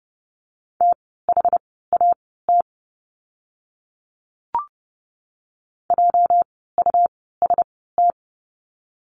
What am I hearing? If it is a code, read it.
Morse code, heard as a keyed tone, about 30 wpm, spaced as for 17 wpm, 700 Hz. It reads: THAT JUST
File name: course / 30:10wpm / 237 Lesson 38f - Letter q - Words 30wpm